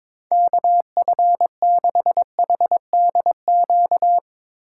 Morse code, heard as a keyed tone, about 23 wpm, 700 Hz. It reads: KF6HDQ
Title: Speed Racing - USA 2x3 Call Signs 22wpm